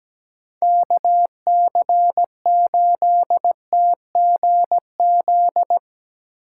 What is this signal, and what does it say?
Morse code, heard as a keyed tone, about 18 wpm, 700 Hz. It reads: KC8TGZ
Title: Speed Racing - USA 2x3 Call Signs 17wpm